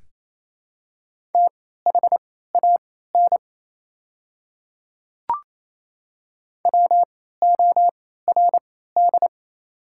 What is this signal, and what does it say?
Morse code, heard as a keyed tone, about 28 wpm, spaced as for 16 wpm, 700 Hz. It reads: THAN WORD
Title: 3x Character Spacing - Top 200 Words - 28wpm